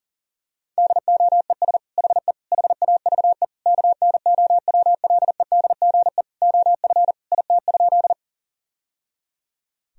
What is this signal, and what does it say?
Morse code, heard as a keyed tone, about 40 wpm, 700 Hz. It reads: DOES HE HAVE KNOWLEDGE OF IT?
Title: Sentences from Top 1000 Words - Review Part 3 - 40wpm